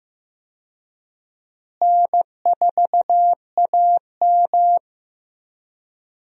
Morse code, heard as a keyed tone, about 15 wpm, 700 Hz. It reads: N4AM
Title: USA Call Signs - 4 Characters - Repeated 3x - 15wpm